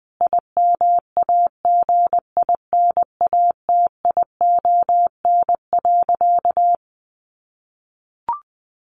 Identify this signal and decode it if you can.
Morse code, heard as a keyed tone, about 20 wpm, 700 Hz. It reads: IMAGINATION.
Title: Sentences from Top 1300 Words 20wpm